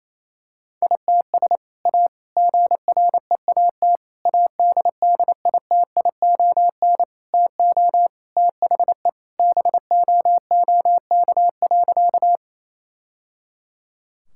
Morse code, heard as a keyed tone, about 28 wpm, 700 Hz. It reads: ITS A GREAT ADDITION TO THE BOOK.
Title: Sentences from Top 1000 Words - Review Part 6 - 28wpm